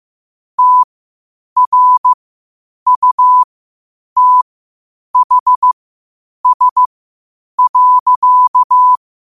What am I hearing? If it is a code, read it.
Morse code, heard as a keyed tone, about 15 wpm, spaced as for 8 wpm, 1000 Hz. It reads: TRUTHS.
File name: qotdfiles / 8wpm